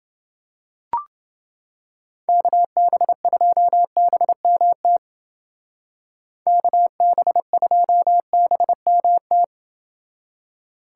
Morse code, heard as a keyed tone, about 28 wpm, 700 Hz. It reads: KB2BMT KB2BMT